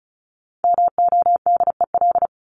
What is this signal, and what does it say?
Morse code, heard as a keyed tone, about 34 wpm, 700 Hz. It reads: MODEL
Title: Top 501-750 Words 35wpm